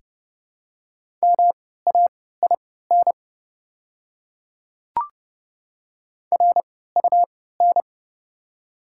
Morse code, heard as a keyed tone, about 30 wpm, spaced as for 17 wpm, 700 Hz. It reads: MAIN RUN
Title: 3x Character Spacing - Top 300 Words - 30wpm